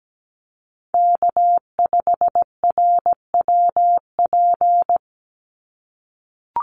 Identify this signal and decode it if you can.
Morse code, heard as a keyed tone, about 17 wpm, 700 Hz. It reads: K5RWP
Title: NA SOTA Chaser Callsigns 17wpm